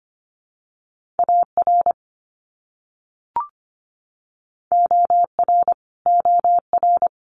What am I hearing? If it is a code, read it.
Morse code, heard as a keyed tone, about 25 wpm, 700 Hz. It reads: AR OR OR
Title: US States Repeated 3x 25wpm